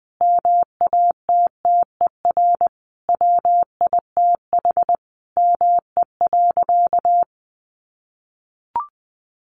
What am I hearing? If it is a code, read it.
Morse code, heard as a keyed tone, about 20 wpm, 700 Hz. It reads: MATTER WITH ME.
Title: Sentences from Top 700 Words 20wpm